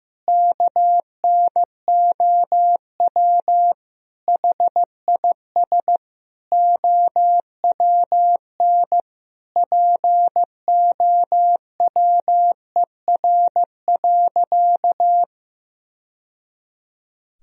Morse code, heard as a keyed tone, about 15 wpm, 700 Hz. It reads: KNOW HIS OWN POWER.
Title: Sentences from Top 1000 Words - Review Part 2 - 15wpm